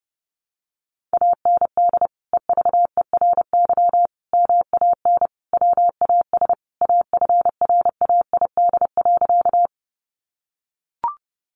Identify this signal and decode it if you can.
Morse code, heard as a keyed tone, about 30 wpm, 700 Hz. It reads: AND EVERY MAN WAS AFRAID.
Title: Sentences from Top 1500 Words 30wpm